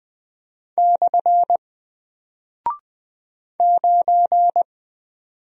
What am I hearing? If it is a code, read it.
Morse code, heard as a keyed tone, about 20 wpm, 700 Hz. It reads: / 9
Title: Single Character (Letter:Number:Punctuation:Prosign) Alt 20wpm